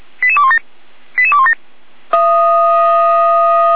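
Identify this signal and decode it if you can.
Five-tone selective call (ZVEI), heard as quick runs of five tones, tones of about 70 ms. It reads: 88217 88217
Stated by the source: SDR receiver recording of ZVEI five-tone selective call, code 88216.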